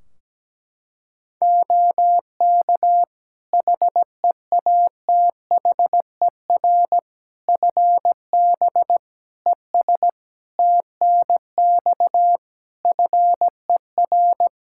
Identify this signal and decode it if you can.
Morse code, heard as a keyed tone, about 17 wpm, 700 Hz. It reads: OK HEATHER FB ES TNX FER
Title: K1USN SST QSO Practice 17wpm